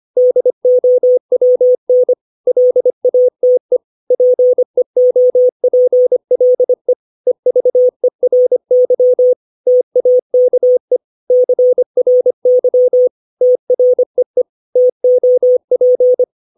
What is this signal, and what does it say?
Morse code, heard as a keyed tone, about 25 wpm, 500 Hz. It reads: DOWN LATE PEOPLE EVERY TAKE CRY TREE TOP
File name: 632 words / 632 Words 25WPM